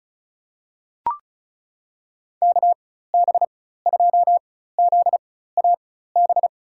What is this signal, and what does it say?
Morse code, heard as a keyed tone, about 36 wpm, spaced as for 16 wpm, 700 Hz. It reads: KB2ZAB
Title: USA Call Signs - 6 Characters - 4x Character Spacing - 35wpm